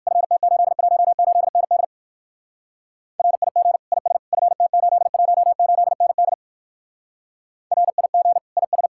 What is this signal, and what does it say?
Morse code, heard as a keyed tone, about 56 wpm, 700 Hz. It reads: FT818ND RIG IS FT818ND RIG IS
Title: Speed-Racing Common QSO Phrases - 45wpm